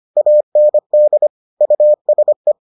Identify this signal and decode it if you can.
Morse code, heard as a keyed tone, about 25 wpm, 600 Hz. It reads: AND USE